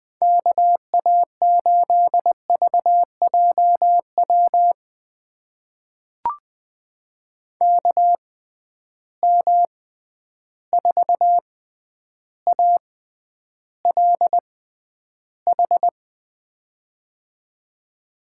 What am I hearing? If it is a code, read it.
Morse code, heard as a keyed tone, about 20 wpm, spaced as for 10 wpm, 700 Hz. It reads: KA8VJW KM4ALH
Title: USA Call Signs - 6 Characters - 6x Character Spacing - 20wpm